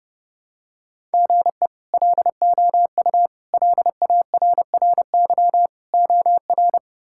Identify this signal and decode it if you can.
Morse code, heard as a keyed tone, about 30 wpm, 700 Hz. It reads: GE LOU LARRY OR
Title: K1USN SST QSO Practice - 30wpm